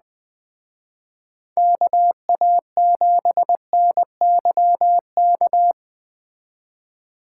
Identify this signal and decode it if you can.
Morse code, heard as a keyed tone, about 19 wpm, 700 Hz. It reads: KA7NYK
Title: Speed Racing - USA 2x3 Call Signs 17wpm